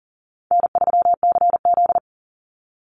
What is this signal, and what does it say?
Morse code, heard as a keyed tone, about 39 wpm, 700 Hz. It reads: N3CZ
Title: Call Signs Repeated 2x - 40wpm